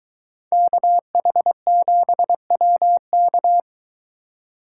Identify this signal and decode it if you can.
Morse code, heard as a keyed tone, about 21 wpm, 700 Hz. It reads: KH7WK